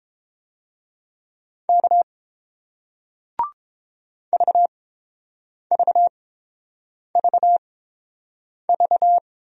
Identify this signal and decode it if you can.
Morse code, heard as a keyed tone, about 31 wpm, spaced as for 7 wpm, 700 Hz. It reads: K VVVV